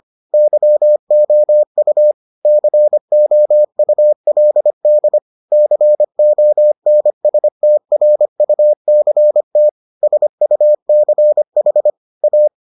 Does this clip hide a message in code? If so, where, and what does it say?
Yes: Morse code from 0.3 to 12.5 s, heard as a keyed tone, about 25 wpm, 600 Hz. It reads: YOU COULD CONSTRUCT SUCH A